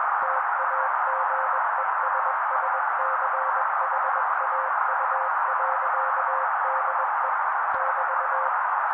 Morse code, heard as a keyed tone, about 21 wpm, 530 Hz. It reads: TAGESSCHAU.DE <BT>